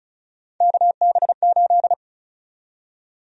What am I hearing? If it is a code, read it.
Morse code, heard as a keyed tone, about 36 wpm, 700 Hz. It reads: KB8
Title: USA Call Sign Prefixes - 35wpm